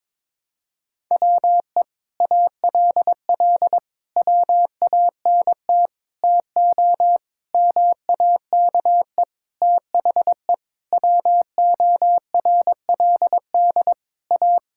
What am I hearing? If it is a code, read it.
Morse code, heard as a keyed tone, about 21 wpm, 700 Hz. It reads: WE ALL WANT TO MAKE THE WORLD A
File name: Sentences with 7-Word Phrases - 22wpm